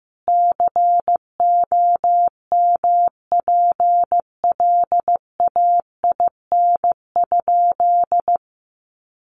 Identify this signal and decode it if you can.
Morse code, heard as a keyed tone, about 14 wpm, 700 Hz. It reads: COMPLAIN?
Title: Sentences from Top 1300 Words 15wpm